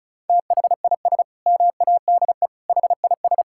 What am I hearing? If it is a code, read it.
Morse code, heard as a keyed tone, about 36 wpm, 700 Hz. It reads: THIS MADE HIS
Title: Sentences from Top 1600 Words 35wpm